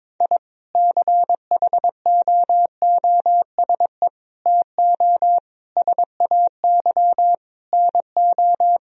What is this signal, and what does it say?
Morse code, heard as a keyed tone, about 22 wpm, 700 Hz. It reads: I CHOOSE TO SAY NO
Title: Sentences from Top 1000 Words - Review Part 4 - 22wpm